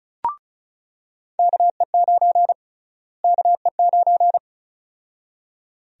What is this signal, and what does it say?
Morse code, heard as a keyed tone, about 36 wpm, 700 Hz. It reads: KE9 KE9
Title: USA Call Sign Prefixes - Repeated 2x - 35wpm